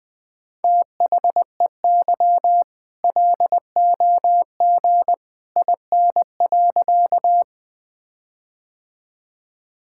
Morse code, heard as a keyed tone, about 20 wpm, 700 Hz. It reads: THEY LOG IN.
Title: All-Age Appropriate Jokes - 20wpm